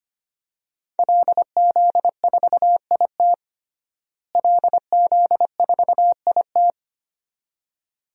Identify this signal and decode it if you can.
Morse code, heard as a keyed tone, about 25 wpm, 700 Hz. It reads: LZ4IT LZ4IT
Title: DXCC-100 Call Signs - 5 Characters - Repeated 2x - 25wpm